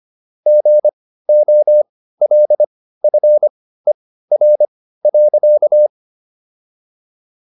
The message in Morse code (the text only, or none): GOLFER.